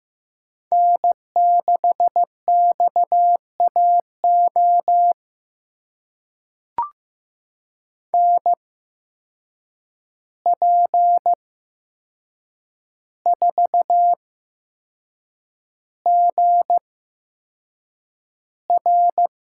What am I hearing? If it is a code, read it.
Morse code, heard as a keyed tone, about 15 wpm, spaced as for 6 wpm, 700 Hz. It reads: N6XAO NP4GR
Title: USA Call Signs - 5 Characters - 8x Character Spacing - 15wpm